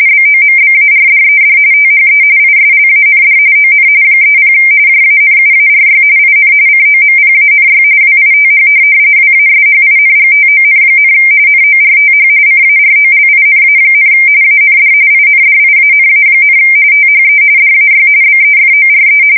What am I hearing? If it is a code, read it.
RTTY, about 45.45 baud. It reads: BOMBARDED THE ISLAND ON 30 JANUARY IN COMPANY WITH WASHINGTON, NORTH CAROLINA, AND INDIANA, THE DAY BEFORE
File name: fsk170 / USS-Massachusetts